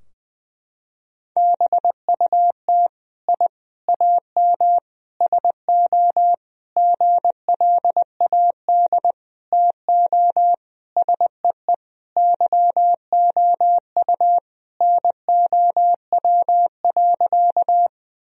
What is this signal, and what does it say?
Morse code, heard as a keyed tone, about 20 wpm, 700 Hz. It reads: BUT I AM SO GLAD TO SEE YOU NOW.
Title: Sentences from Top 1000 Words - Review Part 2 - 20wpm